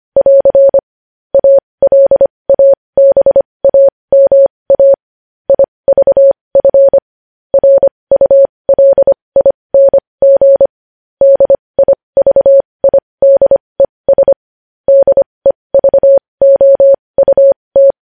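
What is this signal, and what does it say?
Morse code, heard as a keyed tone, about 25 wpm, spaced as for 20 wpm, 550 Hz. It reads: <AR> ALABAMA IVF RULING DIVIDES DEVOUT
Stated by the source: Morse code news headlines practice at 20.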